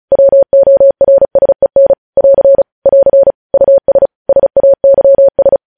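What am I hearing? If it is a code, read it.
Morse code, heard as a keyed tone, about 36 wpm, 550 Hz. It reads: WORSEN <AR> <AR> US SAYS
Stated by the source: Morse code practice sending news headlines at 35 wpm.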